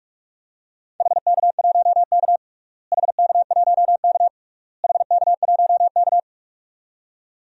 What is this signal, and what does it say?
Morse code, heard as a keyed tone, about 46 wpm, 700 Hz. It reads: HK1K HK1K HK1K
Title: DXCC-100 Call Signs - 4 Characters - Repeated 3x - 45wpm